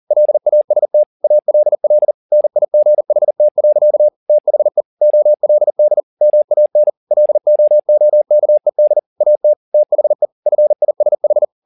Morse code, heard as a keyed tone, about 40 wpm, 600 Hz. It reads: LAST ALL NIGHT. THE OLD MAN LOOKED AT THE FISH
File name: the old man and the sea 2 40wpm